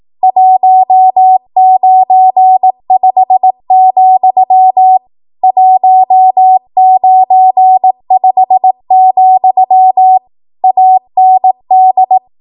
Morse code, heard as a keyed tone, about 18 wpm, 750 Hz. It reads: 195, 195, AND
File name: Morse / 18WPM